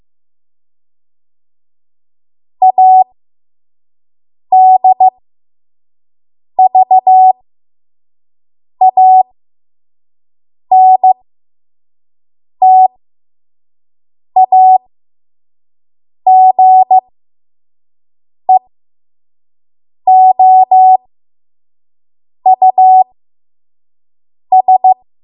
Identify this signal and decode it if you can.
Morse code, heard as a keyed tone, about 15 wpm, spaced as for 5 wpm, 750 Hz. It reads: ADVANTAGEOUS
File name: Archive / 5 WPM / 05WPM